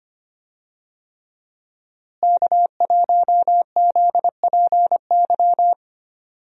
Morse code, heard as a keyed tone, about 25 wpm, 700 Hz. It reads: K1ZPY
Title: USA Call Signs - 5 Characters - Repeated 3x - 25wpm